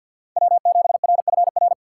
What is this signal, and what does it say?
Morse code, heard as a keyed tone, about 52 wpm, 690 Hz. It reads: W7RFR
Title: USA Call Signs - 5 Characters - 6x Character Spacing - 50wpm